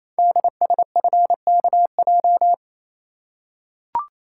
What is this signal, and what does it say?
Morse code, heard as a keyed tone, about 29 wpm, 700 Hz. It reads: DSFKJ
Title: Code Groups - 5 Letters - 28wpm